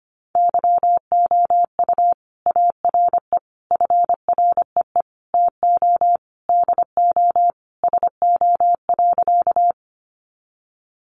Morse code, heard as a keyed tone, about 25 wpm, 700 Hz. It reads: YOU ARE FREE TO DO SO.
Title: Sentences from Top 500 Words 25wpm